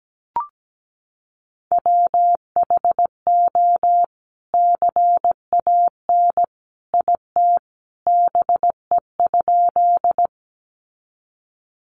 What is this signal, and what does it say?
Morse code, heard as a keyed tone, about 17 wpm, 700 Hz. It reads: WHO CAN IT BE?
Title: Sentences from Top 100 Words 17wpm